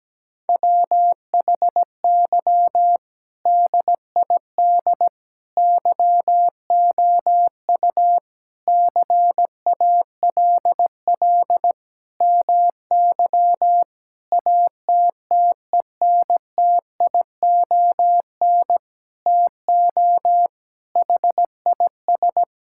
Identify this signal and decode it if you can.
Morse code, heard as a keyed tone, about 17 wpm, 700 Hz. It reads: WHY DID YOU CALL MY ATTENTION TO HIS
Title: Sentences from Top 1000 Words - Review Part 6 - 17wpm